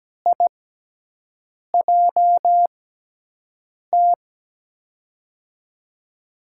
Morse code, heard as a keyed tone, about 17 wpm, spaced as for 6 wpm, 700 Hz. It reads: IJT
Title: USA Call Signs - 5 Characters - 6x Character Spacing - 17wpm